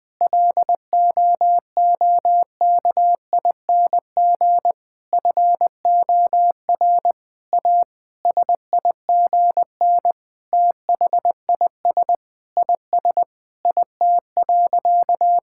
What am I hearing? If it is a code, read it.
Morse code, heard as a keyed tone, about 20 wpm, 700 Hz. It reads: LOOKING FOR A SIGN THIS IS IT.
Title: Sentences with 6-Word Phrases - Spoken Beginning - 20wpm